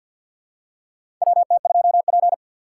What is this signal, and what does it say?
Morse code, heard as a keyed tone, about 52 wpm, 690 Hz. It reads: WT2P
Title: USA Call Signs - 3x Character Spacing - 50wpm